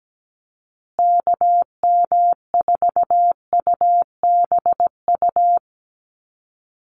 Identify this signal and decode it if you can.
Morse code, heard as a keyed tone, about 17 wpm, 700 Hz. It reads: KM4UBU
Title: USA Call Signs Repeated 3x 17wpm